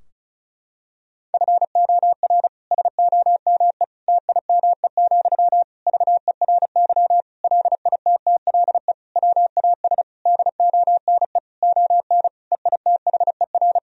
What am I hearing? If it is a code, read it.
Morse code, heard as a keyed tone, about 36 wpm, 700 Hz. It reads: FOR SOME TIME, VERY LITTLE WAS DONE ON EITHER
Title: Sentences from Top 1000 Words - Review Part 4 - 35wpm